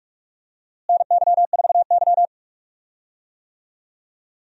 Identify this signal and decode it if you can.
Morse code, heard as a keyed tone, about 46 wpm, 700 Hz. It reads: NY4Y
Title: USA Call Signs - 4 Characters - Repeated 3x - 45wpm